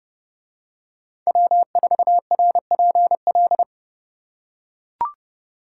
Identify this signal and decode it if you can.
Morse code, heard as a keyed tone, about 28 wpm, 700 Hz. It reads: W4RPL